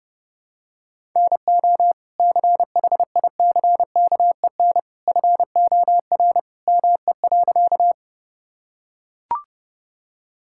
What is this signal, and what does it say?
Morse code, heard as a keyed tone, about 30 wpm, 700 Hz. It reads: NO CHICKEN FOR ME.
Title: Sentences from Top 1000 Words - Review Part 5 - 30wpm